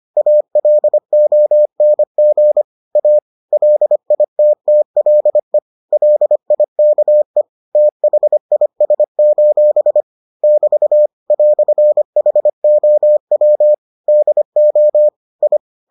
Morse code, heard as a keyed tone, about 25 wpm, 600 Hz. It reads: ALONG A LITTLE LIKE THIS: <BT> "HOW DO I